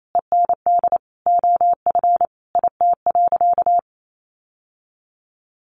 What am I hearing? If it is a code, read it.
Morse code, heard as a keyed tone, about 28 wpm, 700 Hz. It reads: END OF IT.
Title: Sentences from Top 1200 Words 28wpm